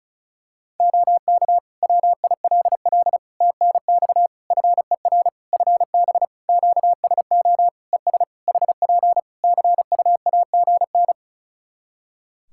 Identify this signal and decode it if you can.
Morse code, heard as a keyed tone, about 36 wpm, 700 Hz. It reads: OK WILL TNX FER FB QSO ES HP CUAGN
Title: Practice QSO - Exchange 4 - Ending - 35wpm